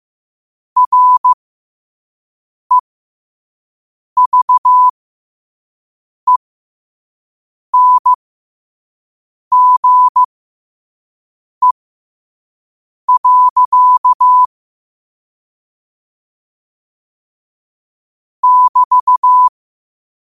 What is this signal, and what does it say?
Morse code, heard as a keyed tone, about 15 wpm, spaced as for 5 wpm, 1000 Hz. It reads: REVENGE. <BT>
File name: qotdfiles / 5wpm